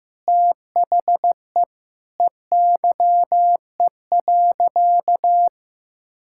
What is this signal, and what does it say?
Morse code, heard as a keyed tone, about 15 wpm, 700 Hz. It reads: THE EYE.
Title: Sentences from Top 1000 Words - Review Part 6 - 15wpm